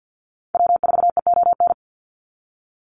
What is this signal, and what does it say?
Morse code, heard as a keyed tone, about 50 wpm, 700 Hz. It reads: W4EON
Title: NA SOTA Activator Callsigns 50wpm